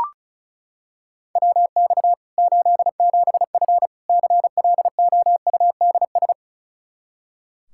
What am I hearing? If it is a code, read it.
Morse code, heard as a keyed tone, about 34 wpm, 700 Hz. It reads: WX 87F CLOUDS
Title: K1USN SST QSO Practice 35wpm